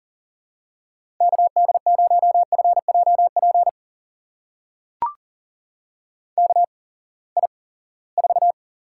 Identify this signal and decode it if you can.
Morse code, heard as a keyed tone, about 40 wpm, spaced as for 23 wpm, 700 Hz. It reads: KD0FJP KI4
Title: USA Call Signs - 6 Characters - 8x Character Spacing - 40wpm